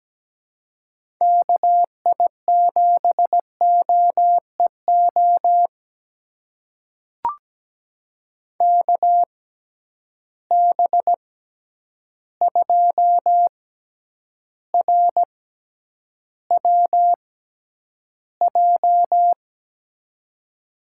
Morse code, heard as a keyed tone, about 17 wpm, spaced as for 9 wpm, 700 Hz. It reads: KI7OEO KB2RWJ
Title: USA Call Signs - 6 Characters - 6x Character Spacing - 17wpm